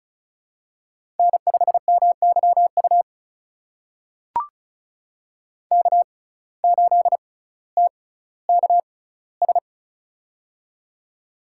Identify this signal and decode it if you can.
Morse code, heard as a keyed tone, about 36 wpm, spaced as for 18 wpm, 700 Hz. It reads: N5MYU K8TKS
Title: USA Call Signs - 5 Characters - 6x Character Spacing - 35wpm